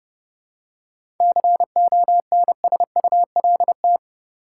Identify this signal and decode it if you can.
Morse code, heard as a keyed tone, about 31 wpm, 700 Hz. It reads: CONSULT
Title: Seven Letter Words - 30wpm